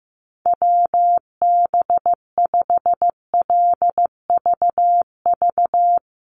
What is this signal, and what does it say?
Morse code, heard as a keyed tone, about 15 wpm, 700 Hz. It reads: WB5LVV
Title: USA Call Signs Repeated 2x 15wpm